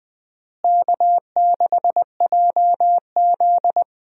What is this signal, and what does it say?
Morse code, heard as a keyed tone, about 20 wpm, 700 Hz. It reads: K6JZ
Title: USA Call Signs - 4 Characters - Repeated 2x - 20wpm